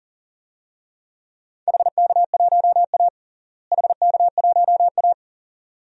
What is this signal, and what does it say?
Morse code, heard as a keyed tone, about 37 wpm, 700 Hz. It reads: HK1A HK1A
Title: DXCC-100 Call Signs - 4 Characters - Repeated 2x - 40wpm